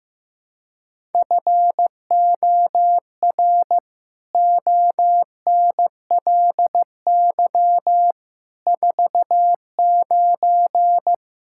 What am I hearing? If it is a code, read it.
Morse code, heard as a keyed tone, about 15 wpm, 700 Hz. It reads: FOR ONLY 49
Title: Pangram Challenge v2 - 15wpm